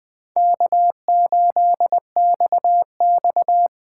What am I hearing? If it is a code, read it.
Morse code, heard as a keyed tone, about 20 wpm, 700 Hz. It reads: K8XX
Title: USA Call Signs - 4 Characters - 3x Character Spacing - 20wpm